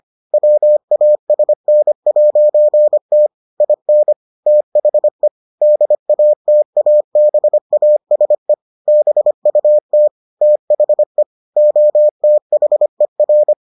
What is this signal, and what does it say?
Morse code, heard as a keyed tone, about 25 wpm, 600 Hz. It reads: WASN'T IN THE DATABASE BUT THE OTHER